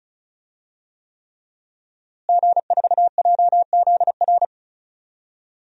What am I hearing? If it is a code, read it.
Morse code, heard as a keyed tone, about 34 wpm, 700 Hz. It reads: G4JZR
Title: DXCC-100 Call Signs - 5 Characters - Repeated 3x - 35wpm